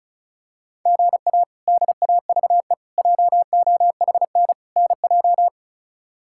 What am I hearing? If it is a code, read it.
Morse code, heard as a keyed tone, about 36 wpm, 700 Hz. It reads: GA DAVE JOHN NJ
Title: K1USN SST QSO Practice - 35wpm